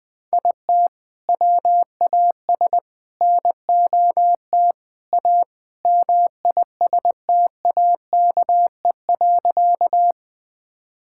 Morse code, heard as a keyed tone, about 20 wpm, 700 Hz. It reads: IT WAS NOT A MISTAKE.
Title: Sentences with 4-Word Phrases - 20wpm